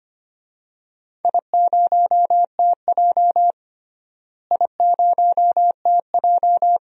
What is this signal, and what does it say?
Morse code, heard as a keyed tone, about 25 wpm, 700 Hz. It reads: I0TJ I0TJ
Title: DXCC-100 Call Signs - 4 Characters - Repeated 3x - 25wpm